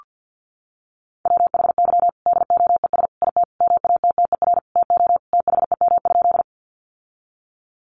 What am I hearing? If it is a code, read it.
Morse code, heard as a keyed tone, about 48 wpm, 700 Hz. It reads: WHY DOES IT MATTER TO THEM?
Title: Sentences from Top 700 Words 50wpm